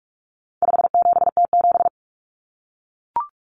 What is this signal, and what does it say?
Morse code, heard as a keyed tone, about 44 wpm, 700 Hz. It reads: 57T7
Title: Common Cut Numbers - 4-Digit 45wpm